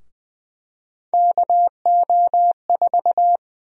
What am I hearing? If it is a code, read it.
Morse code, heard as a keyed tone, about 20 wpm, 700 Hz. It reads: KO4
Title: USA Call Sign Prefixes - Repeated 2x - 20wpm